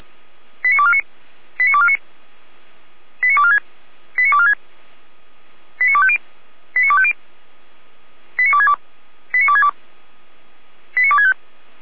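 Five-tone selective call (ZVEI), heard as quick runs of five tones, tones of about 70 ms. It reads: 89249 89249 89246 89246 89250 89250 89262 89262 89265
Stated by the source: ZVEI five-tone selective call received by SDR.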